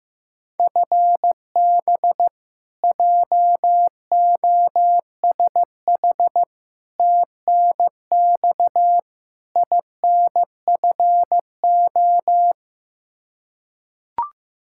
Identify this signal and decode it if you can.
Morse code, heard as a keyed tone, about 16 wpm, 700 Hz. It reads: FB JOSH TNX INFO
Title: Practice QSO - Exchange 3 - More Info - 15wpm